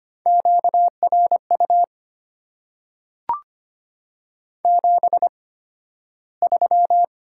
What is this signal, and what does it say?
Morse code, heard as a keyed tone, about 25 wpm, spaced as for 13 wpm, 700 Hz. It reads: QRU 73
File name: Top 90 QSO Elements - 8x Character Spacing - 25wpm